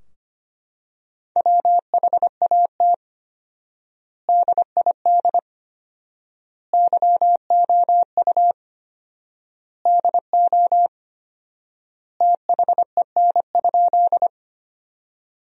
Morse code, heard as a keyed tone, about 25 wpm, 700 Hz. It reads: WHAT DID YOU DO THEN?
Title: 4x Word-Spacing - Sentences from Top 100 Words - 25wpm